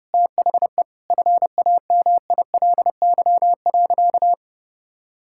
Morse code, heard as a keyed tone, about 30 wpm, 700 Hz. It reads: THE FAMILY.
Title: Sentences from Top 1000 Words - Review Part 5 - 30wpm